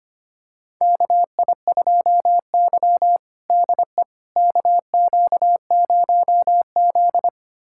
Morse code, heard as a keyed tone, about 25 wpm, 700 Hz. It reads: KI2Y DE KQ0Z
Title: USA Callsign DE Callsign - 4 Characters - 25wpm